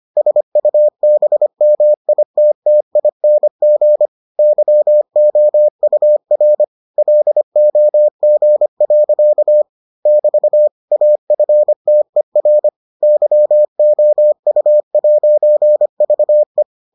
Morse code, heard as a keyed tone, about 25 wpm, 600 Hz. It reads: SUBMITTING YOUR LOG. <BT> AFTER YOU'VE